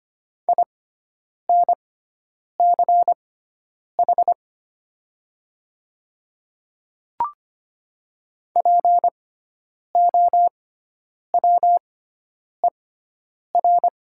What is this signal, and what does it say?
Morse code, heard as a keyed tone, about 25 wpm, spaced as for 9 wpm, 700 Hz. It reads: INCH POWER
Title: 6x Character Spacing - Top 500 Words - 25wpm